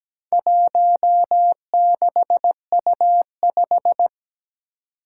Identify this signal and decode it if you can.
Morse code, heard as a keyed tone, about 17 wpm, 700 Hz. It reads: 16U5
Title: Four Challenging Letter-Number Combinations 17wpm